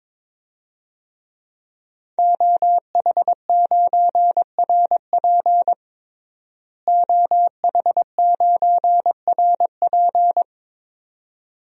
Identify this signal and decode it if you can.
Morse code, heard as a keyed tone, about 21 wpm, 700 Hz. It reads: OH9RP OH9RP
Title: DXCC-100 Call Signs - 5 Characters - Repeated 2x - 22wpm